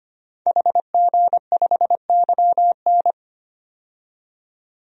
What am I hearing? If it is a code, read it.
Morse code, heard as a keyed tone, about 25 wpm, 700 Hz. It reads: HG5YN
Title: DXCC-100 Call Signs - 5 Characters - 25wpm